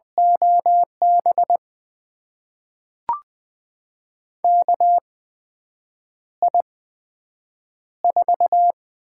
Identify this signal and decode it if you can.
Morse code, heard as a keyed tone, about 20 wpm, spaced as for 7 wpm, 700 Hz. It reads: OB KI4
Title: USA Call Signs - 6 Characters - 8x Character Spacing - 20wpm